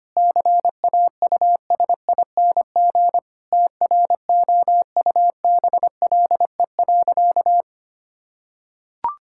Morse code, heard as a keyed tone, about 25 wpm, 700 Hz. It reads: CAUSING TROUBLE.
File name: practice / 25wpm / Sentences with 9-Word Phrases - 25wpm